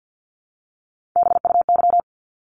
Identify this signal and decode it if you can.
Morse code, heard as a keyed tone, about 50 wpm, 700 Hz. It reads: BUY
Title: Top 501-750 Words 50wpm